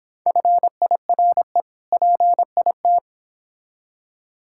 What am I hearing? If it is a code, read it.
Morse code, heard as a keyed tone, about 26 wpm, 700 Hz. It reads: FIRE PIT